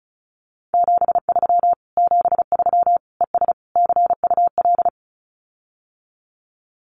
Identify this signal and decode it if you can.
Morse code, heard as a keyed tone, about 36 wpm, 700 Hz. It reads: 73 73 ES CUL
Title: Common QSO Phrases Encore 35wpm